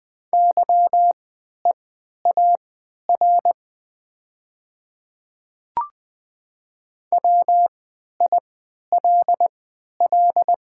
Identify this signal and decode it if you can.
Morse code, heard as a keyed tone, about 20 wpm, spaced as for 11 wpm, 700 Hz. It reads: YEAR WILL
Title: 3x Character Spacing - Top 200 Words - 20wpm